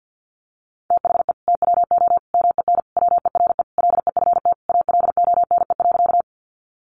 Morse code, heard as a keyed tone, about 50 wpm, 700 Hz. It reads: THE TWO MEN WERE LEFT ALONE.
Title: Sentences from Top 1000 Words 50wpm